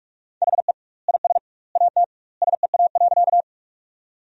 Morse code, heard as a keyed tone, about 46 wpm, 700 Hz. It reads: HE IS AT SEA.
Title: 2x Word-Spacing - Sentences from Top 300 Words - 45wpm